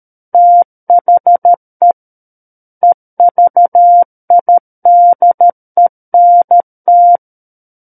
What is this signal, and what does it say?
Morse code, heard as a keyed tone, about 13 wpm, 700 Hz. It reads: THE EVIDENT
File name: Book One / 04-Chapter 4 - 13 WPM